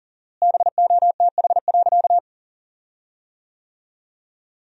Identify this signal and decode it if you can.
Morse code, heard as a keyed tone, about 37 wpm, 700 Hz. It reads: BOTH.
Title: 4x Word-Spacing - Sentences from Top 300 Words - 40wpm